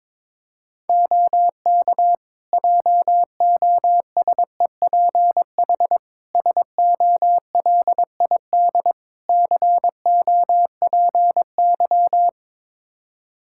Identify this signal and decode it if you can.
Morse code, heard as a keyed tone, about 21 wpm, 700 Hz. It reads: OK JOSEPH SOLID COPY
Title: Practice QSO - Exchange 3 - More Info - 22wpm